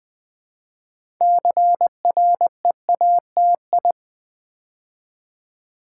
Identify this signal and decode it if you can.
Morse code, heard as a keyed tone, about 20 wpm, 700 Hz. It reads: CREATI
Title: Build A Word - 3 Letters At A Time - 8-Letter Words - 20wpm